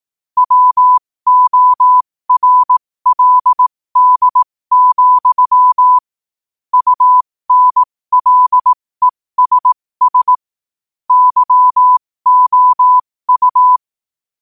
Morse code, heard as a keyed tone, about 18 wpm, spaced as for 15 wpm, 1000 Hz. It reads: WORLD, UNLESS YOU
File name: qotdfiles / 15wpm